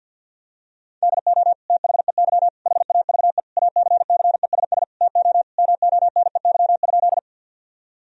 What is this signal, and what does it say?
Morse code, heard as a keyed tone, about 50 wpm, 700 Hz. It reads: DO THEY HAVE ACCESS TO MONEY?
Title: Sentences from Top 1000 Words - Review Part 4 - 50wpm